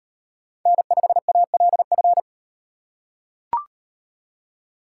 Morse code, heard as a keyed tone, about 38 wpm, 700 Hz. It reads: N5ALF